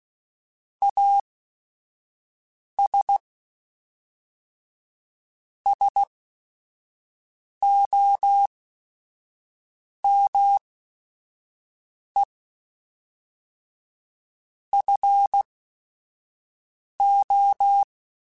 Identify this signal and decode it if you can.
Morse code, heard as a keyed tone, about 16 wpm, spaced as for 5 wpm, 800 Hz. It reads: AS SOME FO
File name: Technical Subjects / Technical Subjects-mp3s / Sw Pwr Sup @5